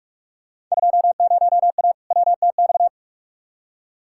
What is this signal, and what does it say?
Morse code, heard as a keyed tone, about 41 wpm, 700 Hz. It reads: 20A WTX
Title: Field Day Exchanges Encore - 45wpm